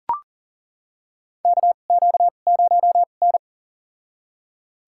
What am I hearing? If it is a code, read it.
Morse code, heard as a keyed tone, about 40 wpm, spaced as for 29 wpm, 700 Hz. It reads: KQ0N
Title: USA Call Signs - 4 Characters - 2x Character Spacing - 40wpm